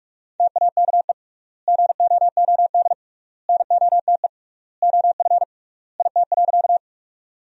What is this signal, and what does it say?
Morse code, heard as a keyed tone, about 46 wpm, 700 Hz. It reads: TAKE GOOD NOTE OF IT.
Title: 3x Word-Spacing - Sentences from Top 500 Words - 45wpm